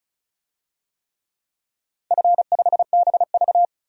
Morse code, heard as a keyed tone, about 36 wpm, 700 Hz. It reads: F5BV
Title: DXCC-100 Call Signs - 4 Characters - 35wpm